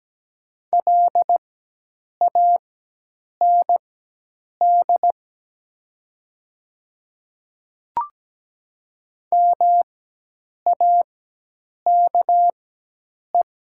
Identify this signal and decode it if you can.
Morse code, heard as a keyed tone, about 17 wpm, spaced as for 8 wpm, 700 Hz. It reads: LAND MAKE
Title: 4x Character Spacing - Top 300 Words - 17wpm